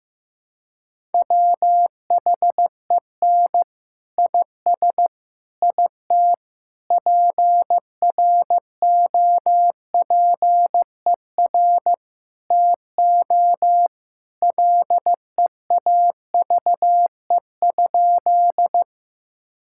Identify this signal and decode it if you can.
Morse code, heard as a keyed tone, about 15 wpm, 700 Hz. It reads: WHEN IS IT PROPER TO LEAVE?
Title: Sentences from Top 1000 Words - Review Part 5 - 15wpm